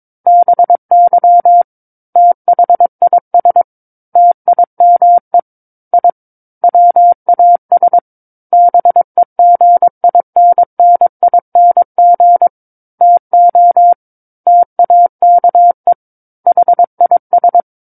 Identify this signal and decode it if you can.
Morse code, heard as a keyed tone, about 22 wpm, 700 Hz. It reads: BY THIS TIME I WAS BEGINNING TO TAKE HIS
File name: Book One / 13-Chapter 13 - 22 WPM